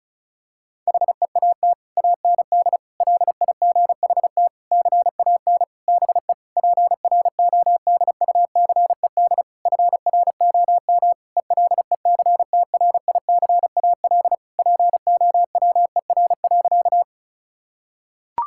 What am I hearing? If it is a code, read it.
Morse code, heard as a keyed tone, about 34 wpm, 700 Hz. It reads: HEAT AND LIGHT CAN BE PRODUCED FROM ELECTRICAL POWER.
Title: Sentences from Top 1000 Words - Review Part 6 - 35wpm